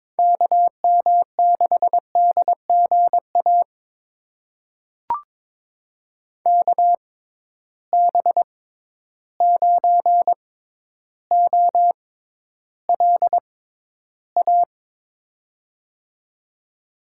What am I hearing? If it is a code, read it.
Morse code, heard as a keyed tone, about 22 wpm, spaced as for 11 wpm, 700 Hz. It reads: KM6DGA KB9OLA
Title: USA Call Signs - 6 Characters - 6x Character Spacing - 22wpm